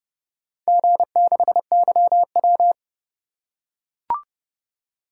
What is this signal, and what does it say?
Morse code, heard as a keyed tone, about 30 wpm, 700 Hz. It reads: G6YW